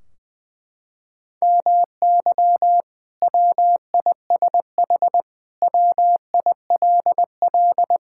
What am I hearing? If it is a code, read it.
Morse code, heard as a keyed tone, about 20 wpm, 700 Hz. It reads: MY WISH WILL
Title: Sentences from Top 1600 Words 20wpm